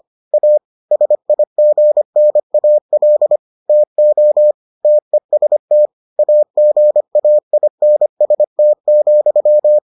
Morse code, heard as a keyed tone, about 25 wpm, 600 Hz. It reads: A SIGNAL TO TEST AGAINST,